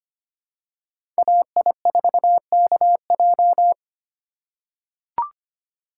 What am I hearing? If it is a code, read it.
Morse code, heard as a keyed tone, about 25 wpm, 700 Hz. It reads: AI4KJ
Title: USA Call Signs - 5 Characters - 2x Character Spacing - 25wpm